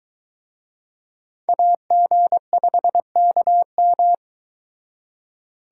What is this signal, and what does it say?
Morse code, heard as a keyed tone, about 22 wpm, 700 Hz. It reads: AG5KM